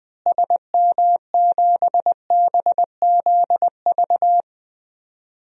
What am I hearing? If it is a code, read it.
Morse code, heard as a keyed tone, about 20 wpm, 700 Hz. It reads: SM7BZV
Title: DXCC-100 Call Signs - 6 Characters - Repeated 3x - 20wpm